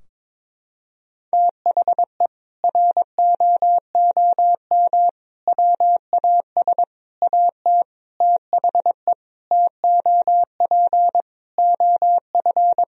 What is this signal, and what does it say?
Morse code, heard as a keyed tone, about 22 wpm, 700 Hz. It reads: THE ROOM WAS AT THE TOP OF
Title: Sentences from Top 1000 Words - Review Part 2 - 22wpm